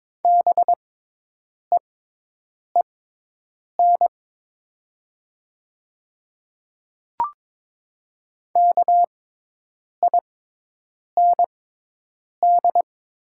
Morse code, heard as a keyed tone, about 21 wpm, spaced as for 8 wpm, 690 Hz. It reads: BEEN KIND
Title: 6x Character Spacing - Top 200 Words - 22wpm